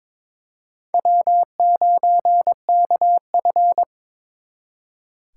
Morse code, heard as a keyed tone, about 26 wpm, 700 Hz. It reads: W9KF
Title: Speed Racing - USA 1x2 Call Signs 22wpm